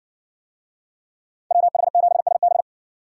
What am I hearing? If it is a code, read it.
Morse code, heard as a keyed tone, about 60 wpm, 700 Hz. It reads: WH7SB